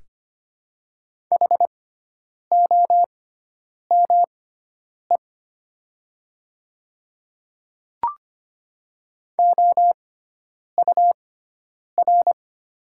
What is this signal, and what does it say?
Morse code, heard as a keyed tone, about 25 wpm, spaced as for 9 wpm, 700 Hz. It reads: HOME OUR
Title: 6x Character Spacing - Top 200 Words - 25wpm